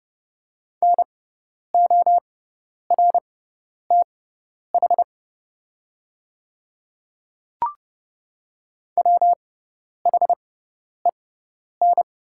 Morse code, heard as a keyed tone, about 31 wpm, spaced as for 10 wpm, 700 Hz. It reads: NORTH WHEN
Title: 6x Character Spacing - Top 300 Words - 30wpm